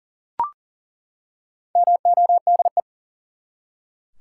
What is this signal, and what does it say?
Morse code, heard as a keyed tone, about 40 wpm, 700 Hz. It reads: MODE
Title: Four Letter Words - 40wpm